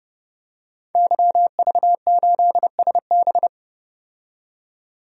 Morse code, heard as a keyed tone, about 30 wpm, 700 Hz. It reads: YV8SB
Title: DXCC-100 Call Signs - 5 Characters - 30wpm